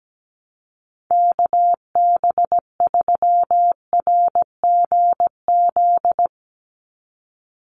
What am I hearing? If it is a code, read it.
Morse code, heard as a keyed tone, about 17 wpm, 700 Hz. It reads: KB3RGZ
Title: USA Call Signs 17wpm